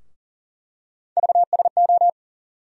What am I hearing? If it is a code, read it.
Morse code, heard as a keyed tone, about 40 wpm, 700 Hz. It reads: VSO
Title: USA Call Sign Postfixes - 3 Letters 40wpm